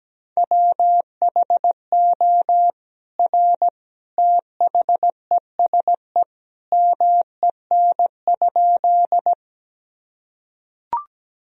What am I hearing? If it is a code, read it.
Morse code, heard as a keyed tone, about 17 wpm, 700 Hz. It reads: WHO R THESE MEN?
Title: Sentences from Top 200 Words and Abbreviations 17wpm